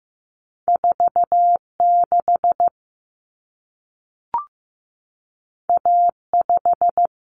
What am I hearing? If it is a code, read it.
Morse code, heard as a keyed tone, about 15 wpm, 700 Hz. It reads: 46 A5
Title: Common Cut Numbers - 2-Digit 15wpm